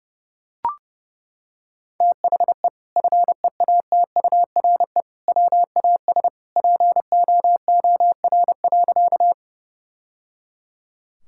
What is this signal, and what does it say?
Morse code, heard as a keyed tone, about 30 wpm, 700 Hz. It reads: THE FEATURE WAS POOR.
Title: Sentences from Top 1000 Words - Review Part 5 - 30wpm